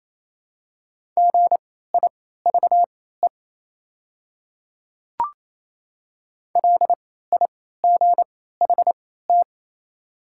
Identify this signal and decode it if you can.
Morse code, heard as a keyed tone, about 28 wpm, spaced as for 16 wpm, 700 Hz. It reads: GIVE LIGHT
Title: 3x Character Spacing - Top 300 Words - 28wpm